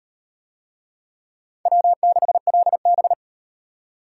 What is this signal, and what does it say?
Morse code, heard as a keyed tone, about 36 wpm, 700 Hz. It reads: W6LB